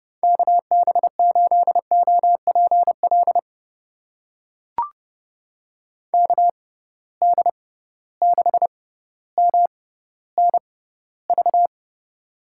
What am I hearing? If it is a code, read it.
Morse code, heard as a keyed tone, about 30 wpm, spaced as for 15 wpm, 700 Hz. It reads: KB8OPL KD6MNV
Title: USA Call Signs - 6 Characters - 6x Character Spacing - 30wpm